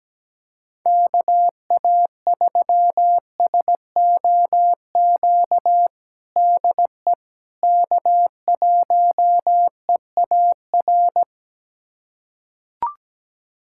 Morse code, heard as a keyed tone, about 17 wpm, 700 Hz. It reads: KA3SOQ DE K1EAR
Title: USA Callsign DE Callsign 17wpm